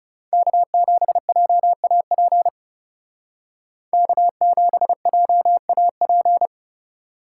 Morse code, heard as a keyed tone, about 35 wpm, 700 Hz. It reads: K7JAP K7JAP
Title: Speed Racing - USA 1x3 Call Signs 30wpm